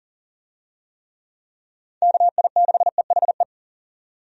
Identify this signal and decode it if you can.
Morse code, heard as a keyed tone, about 40 wpm, 700 Hz. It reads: KI6EHE
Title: USA Call Signs - 6 Characters - 40wpm